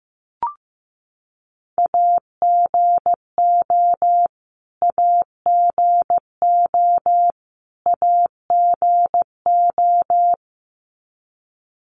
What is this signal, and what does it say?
Morse code, heard as a keyed tone, about 15 wpm, 700 Hz. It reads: AGO AGO AGO
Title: Top 500 Words Repeated 15wpm